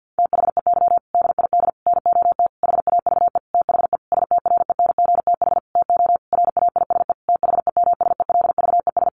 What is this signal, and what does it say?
Morse code, heard as a keyed tone, about 50 wpm, 700 Hz. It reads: THEY DID NOT HAVE THE STRENGTH TO RAISE THEMSELVES
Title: Sentences from Top 1400 Words 50wpm